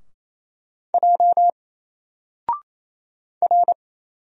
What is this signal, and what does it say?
Morse code, heard as a keyed tone, about 27 wpm, 700 Hz. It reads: J R
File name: Single-Letter-Number-28wpm